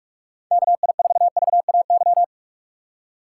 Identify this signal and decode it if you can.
Morse code, heard as a keyed tone, about 46 wpm, 700 Hz. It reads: KI4VAY
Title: USA Call Signs - 6 Characters - 2x Character Spacing - 45wpm